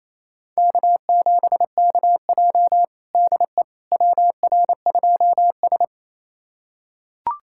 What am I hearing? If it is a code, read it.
Morse code, heard as a keyed tone, about 28 wpm, 700 Hz. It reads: K7KJ DE WR2S
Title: USA Callsign DE Callsign - 4 Characters - 28wpm